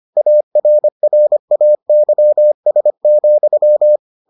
Morse code, heard as a keyed tone, about 25 wpm, 600 Hz. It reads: ARRAYS,